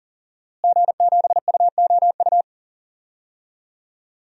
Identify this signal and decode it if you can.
Morse code, heard as a keyed tone, about 40 wpm, 700 Hz. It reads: G7UOU
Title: DXCC-100 Call Signs - 5 Characters - Repeated 3x - 40wpm